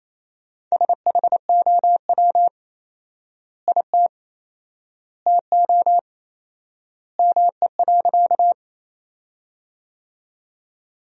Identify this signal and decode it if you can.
Morse code, heard as a keyed tone, about 28 wpm, 700 Hz. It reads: SHOW IT TO ME.
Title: 4x Word-Spacing - Sentences from Top 200 Words - 28wpm